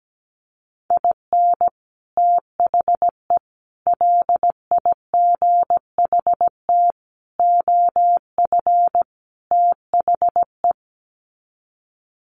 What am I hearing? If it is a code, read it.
Morse code, heard as a keyed tone, about 17 wpm, 700 Hz. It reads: IN THE LIGHT OF THE
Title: Sets of 5 Words Encore2 17wpm